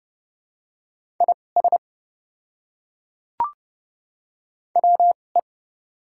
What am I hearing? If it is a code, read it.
Morse code, heard as a keyed tone, about 30 wpm, spaced as for 22 wpm, 700 Hz. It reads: IS WE
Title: 2x Character Spacing - Top 100 Words - 30wpm